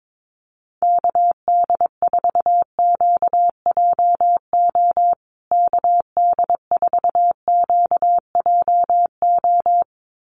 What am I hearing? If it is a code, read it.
Morse code, heard as a keyed tone, about 22 wpm, 700 Hz. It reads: KD4QJO KD4QJO
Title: USA Call Signs Repeated 2x 22wpm